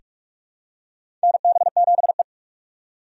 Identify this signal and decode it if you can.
Morse code, heard as a keyed tone, about 43 wpm, 700 Hz. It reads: NB7E
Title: USA Call Signs - 4 Characters - 6x Character Spacing - 45wpm